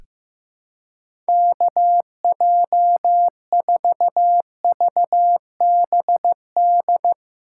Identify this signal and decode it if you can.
Morse code, heard as a keyed tone, about 15 wpm, 700 Hz. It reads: KJ4VBD
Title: USA Callsign DE Callsign 15wpm